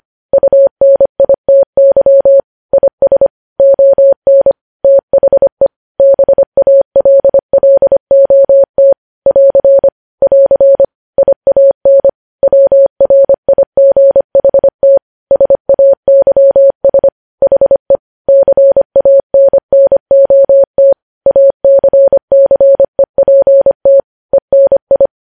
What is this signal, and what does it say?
Morse code, heard as a keyed tone, about 25 wpm, 550 Hz. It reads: UNITY IS ON THE BALLOT <AR> <AR> IAN WRIGHT SAYS HE CANNOT ACCEPT ENI
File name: podcast / News headlines-1x-25x25